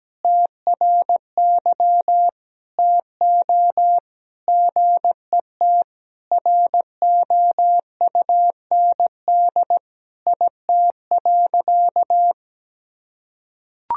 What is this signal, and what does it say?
Morse code, heard as a keyed tone, about 17 wpm, 700 Hz. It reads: TRY TO GET ROUND IT.
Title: Sentences from Top 1000 Words - Review Part 1 - 17wpm